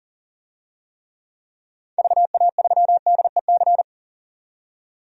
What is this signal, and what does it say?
Morse code, heard as a keyed tone, about 40 wpm, 700 Hz. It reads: VA3DEC
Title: DXCC-100 Call Signs - 6 Characters - 40wpm